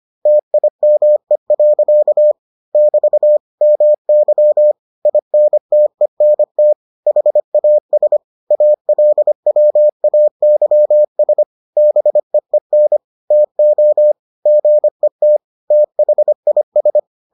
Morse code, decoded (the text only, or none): TIME. <BT> MY INTENT HAS ALWAYS BEEN TO GET THIS